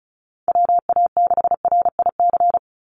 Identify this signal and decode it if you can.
Morse code, heard as a keyed tone, about 34 wpm, 700 Hz. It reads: WA6RIC
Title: NA SOTA Chaser Callsigns 35wpm